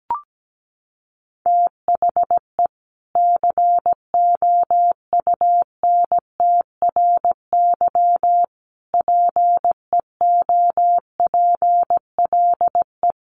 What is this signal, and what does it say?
Morse code, heard as a keyed tone, about 17 wpm, 700 Hz. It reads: THE COUNTRY PEOPLE
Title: Sentences from Top 1100 Words 17wpm